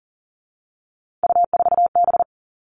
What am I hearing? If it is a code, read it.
Morse code, heard as a keyed tone, about 40 wpm, 700 Hz. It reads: U4B
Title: Three Challenging Letter-Number Combinations 40wpm